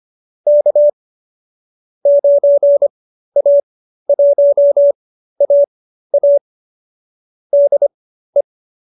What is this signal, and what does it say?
Morse code, heard as a keyed tone, about 25 wpm, spaced as for 13 wpm, 570 Hz. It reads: K 9A1AA DE